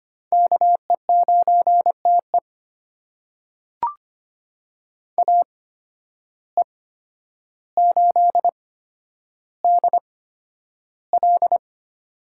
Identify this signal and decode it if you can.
Morse code, heard as a keyed tone, about 25 wpm, spaced as for 11 wpm, 700 Hz. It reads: KE9TE AE8DL
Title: USA Call Signs - 5 Characters - 8x Character Spacing - 25wpm